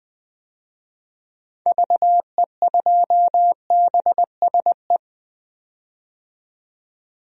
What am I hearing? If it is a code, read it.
Morse code, heard as a keyed tone, about 21 wpm, 700 Hz. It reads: VE2BSE
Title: DXCC-100 Call Signs - 6 Characters - 20wpm